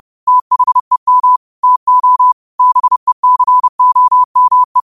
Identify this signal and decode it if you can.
Morse code, heard as a keyed tone, about 30 wpm, 1000 Hz. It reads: THEM TO BECOME
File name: qotdfiles / 30wpm